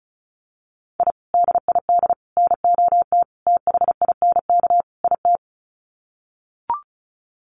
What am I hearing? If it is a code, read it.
Morse code, heard as a keyed tone, about 34 wpm, 700 Hz. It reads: I DID NOT THINK IT
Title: Sets of 5 Words Encore2 35wpm